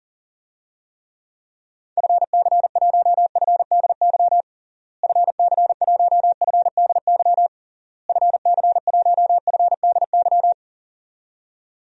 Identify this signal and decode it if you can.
Morse code, heard as a keyed tone, about 40 wpm, 700 Hz. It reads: FC1FDY FC1FDY FC1FDY
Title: DXCC-100 Call Signs - 6 Characters - Repeated 3x - 40wpm